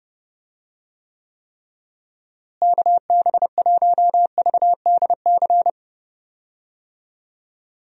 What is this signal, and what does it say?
Morse code, heard as a keyed tone, about 31 wpm, 700 Hz. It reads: KB1VDC
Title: USA Call Signs - 6 Characters - Repeated 3x - 30wpm